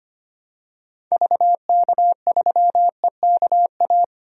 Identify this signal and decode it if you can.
Morse code, heard as a keyed tone, about 25 wpm, 700 Hz. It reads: VK3EKA
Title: DXCC-100 Call Signs - 6 Characters - Repeated 2x - 25wpm